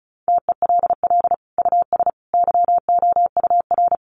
Morse code, heard as a keyed tone, about 34 wpm, 700 Hz. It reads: TELL US YOUR
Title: Sentences from Top 1200 Words 35wpm